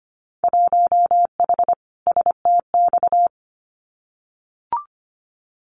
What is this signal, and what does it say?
Morse code, heard as a keyed tone, about 25 wpm, 700 Hz. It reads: 1H STX
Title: Winter Field Day Exchanges 25wpm